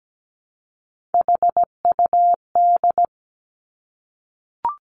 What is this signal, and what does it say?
Morse code, heard as a keyed tone, about 17 wpm, 700 Hz. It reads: HUD
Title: Three Challenging Letter-Number Combinations 17wpm